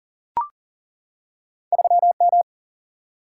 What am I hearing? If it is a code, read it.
Morse code, heard as a keyed tone, about 39 wpm, 700 Hz. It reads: 3 M
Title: Single-Letter Single-Number 40wpm